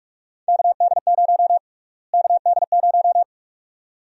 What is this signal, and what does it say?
Morse code, heard as a keyed tone, about 45 wpm, 700 Hz. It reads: KD0 KD0
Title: USA Call Sign Prefixes - Repeated 2x - 45wpm